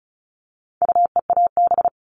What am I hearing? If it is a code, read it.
Morse code, heard as a keyed tone, about 34 wpm, 700 Hz. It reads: UEAB
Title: All Cut Numbers - 4-Digit 35wpm